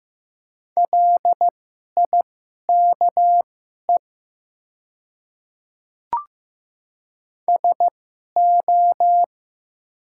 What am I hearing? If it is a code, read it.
Morse code, heard as a keyed tone, about 15 wpm, spaced as for 11 wpm, 700 Hz. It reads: LIKE SO
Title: 2x Character Spacing - Top 100 Words - 15wpm